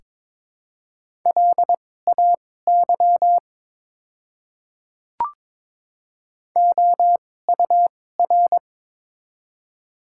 Morse code, heard as a keyed tone, about 22 wpm, spaced as for 16 wpm, 700 Hz. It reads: LAY OUR
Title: 2x Character Spacing - Top 500 Words - 22wpm